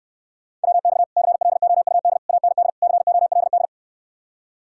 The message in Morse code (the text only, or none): WX CLOUD AND COLD